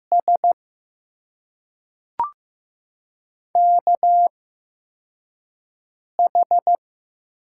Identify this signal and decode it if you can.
Morse code, heard as a keyed tone, about 15 wpm, spaced as for 3 wpm, 700 Hz. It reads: SKH
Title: USA Call Signs - 8x Character Spacing - 15wpm